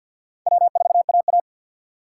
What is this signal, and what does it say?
Morse code, heard as a keyed tone, about 64 wpm, spaced as for 57 wpm, 700 Hz. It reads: W4AA
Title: Speed Racing - USA 1x2 Call Signs 50wpm